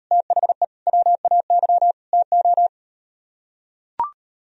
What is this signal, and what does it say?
Morse code, heard as a keyed tone, about 38 wpm, 700 Hz. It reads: THE WAY TO